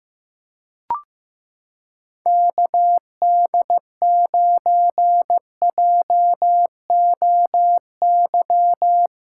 Morse code, heard as a keyed tone, about 15 wpm, 700 Hz. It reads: KD9JOY
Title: USA Callsign DE Callsign - 6 Characters - 15wpm